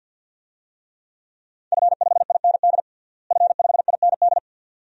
Morse code, heard as a keyed tone, about 49 wpm, 700 Hz. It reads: F5IND F5IND
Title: DXCC-100 Call Signs - 5 Characters - Repeated 2x - 50wpm